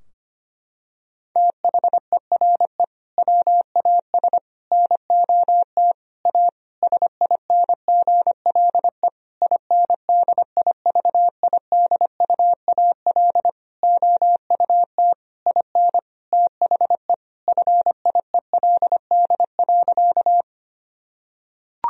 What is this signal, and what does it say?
Morse code, heard as a keyed tone, about 25 wpm, 700 Hz. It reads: THERE WAS NOT A SINGLE INDIVIDUAL OUT IN THE FIELD.
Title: Sentences from Top 1000 Words - Review Part 4 - 25wpm